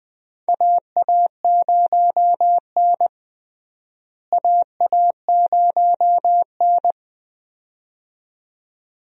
Morse code, heard as a keyed tone, about 20 wpm, 700 Hz. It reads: AA0N AA0N
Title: USA Call Signs - 4 Characters - Repeated 2x - 20wpm